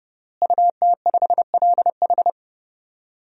Morse code, heard as a keyed tone, about 32 wpm, 700 Hz. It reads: UT5LH
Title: DXCC-100 Call Signs - 5 Characters - Repeated 3x - 30wpm